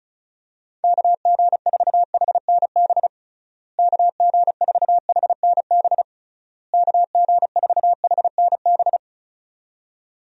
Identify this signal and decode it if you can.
Morse code, heard as a keyed tone, about 34 wpm, 700 Hz. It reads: KG4HNB KG4HNB KG4HNB
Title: USA Call Signs - 6 Characters - Repeated 3x - 35wpm